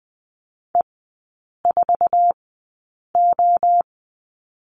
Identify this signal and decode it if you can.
Morse code, heard as a keyed tone, about 20 wpm, 700 Hz. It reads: E 4 O
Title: 133 Lesson 22e - Question Mark - Three Characters 20wpm